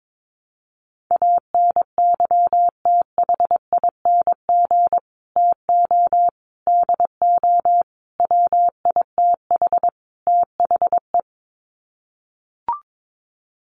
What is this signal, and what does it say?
Morse code, heard as a keyed tone, about 22 wpm, 700 Hz. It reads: ANYTHING TO DO WITH THE
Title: Sets of 5 Words Encore 22wpm